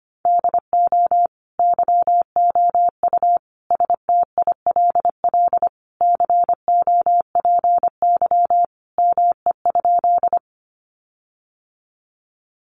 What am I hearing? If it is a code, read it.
Morse code, heard as a keyed tone, about 25 wpm, 700 Hz. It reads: DO YOU STILL COPY ME?
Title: Sentences from Top 1400 Words 25wpm